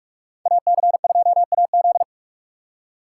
Morse code, heard as a keyed tone, about 58 wpm, 700 Hz. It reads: AC2AZ